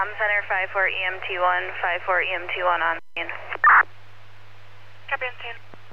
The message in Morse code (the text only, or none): none